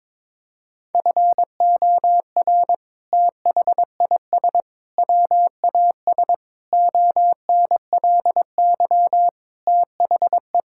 Morse code, heard as a keyed tone, about 22 wpm, 700 Hz. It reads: FOR THIS WAS ONLY THE
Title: Sentences from Top 1000 Words - Review Part 5 - 22wpm